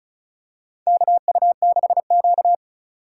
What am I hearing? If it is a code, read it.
Morse code, heard as a keyed tone, about 34 wpm, 700 Hz. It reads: KU6Q
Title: USA Call Signs - 4 Characters - 3x Character Spacing - 35wpm